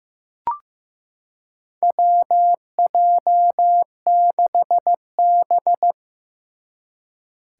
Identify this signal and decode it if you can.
Morse code, heard as a keyed tone, about 15 wpm, 700 Hz. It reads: WJ6B
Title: Four Challenging Letter-Number Combinations 15wpm